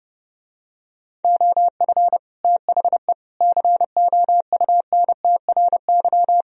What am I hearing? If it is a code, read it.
Morse code, heard as a keyed tone, about 30 wpm, 700 Hz. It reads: OF THE COUNTRY